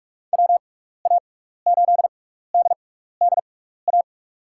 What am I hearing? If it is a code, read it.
Morse code, heard as a keyed tone, about 46 wpm, spaced as for 15 wpm, 700 Hz. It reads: WA8DDA
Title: USA Call Signs - 6x Character Spacing - 45wpm